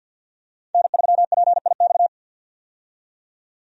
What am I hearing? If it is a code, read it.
Morse code, heard as a keyed tone, about 51 wpm, 700 Hz. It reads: N3PIX
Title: USA Call Signs - 5 Characters - 50wpm